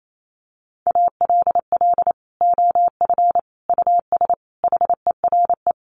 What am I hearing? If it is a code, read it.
Morse code, heard as a keyed tone, about 28 wpm, 700 Hz. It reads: ALL OF US HERE
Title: Sets of 5 Words 28wpm